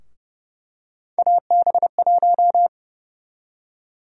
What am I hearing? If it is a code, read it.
Morse code, heard as a keyed tone, about 32 wpm, 700 Hz. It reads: AB1
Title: USA Call Sign Prefixes - 30wpm